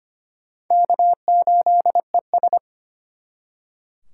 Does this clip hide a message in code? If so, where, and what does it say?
Yes: Morse code from 0.7 to 2.6 s, heard as a keyed tone, about 33 wpm, 700 Hz. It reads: K8ES